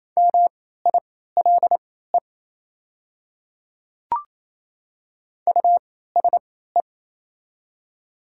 Morse code, heard as a keyed tone, about 27 wpm, spaced as for 16 wpm, 690 Hz. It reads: MILE USE
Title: 3x Character Spacing - Top 300 Words - 28wpm